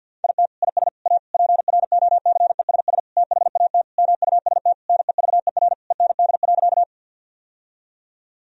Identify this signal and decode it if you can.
Morse code, heard as a keyed tone, about 48 wpm, 700 Hz. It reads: IT IS A PROCESS THAT MUST NEVER END.
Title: Sentences from Top 1000 Words - Review Part 3 - 50wpm